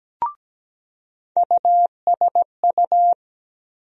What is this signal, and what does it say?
Morse code, heard as a keyed tone, about 17 wpm, 700 Hz. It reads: USU
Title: USA Call Sign Postfixes - 3 Letters 17wpm